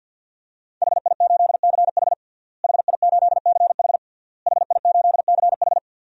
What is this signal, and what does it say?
Morse code, heard as a keyed tone, about 50 wpm, 700 Hz. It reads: HI8CH HI8CH HI8CH
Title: DXCC-100 Call Signs - 5 Characters - Repeated 3x - 50wpm